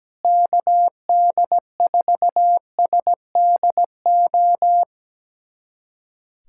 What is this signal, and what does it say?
Morse code, heard as a keyed tone, about 17 wpm, 700 Hz. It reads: KD4SDO
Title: USA Callsign DE Callsign 17wpm